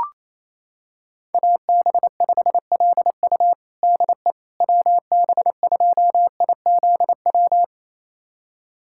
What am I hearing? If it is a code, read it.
Morse code, heard as a keyed tone, about 28 wpm, 700 Hz. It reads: AB5LU DE WB2IZW
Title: USA Callsign DE Callsign 28wpm